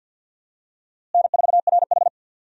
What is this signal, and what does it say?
Morse code, heard as a keyed tone, about 52 wpm, 700 Hz. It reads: N4RH
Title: USA Call Signs - 4 Characters - Repeated 2x - 50wpm